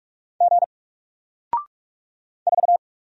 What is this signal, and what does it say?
Morse code, heard as a keyed tone, about 44 wpm, 700 Hz. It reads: G 4
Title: Single Letter-Number 45wpm